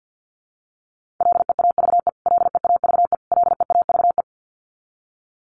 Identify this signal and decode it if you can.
Morse code, heard as a keyed tone, about 50 wpm, 700 Hz. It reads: LEAVE LEAVE LEAVE
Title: Top 500 Words Repeated 50wpm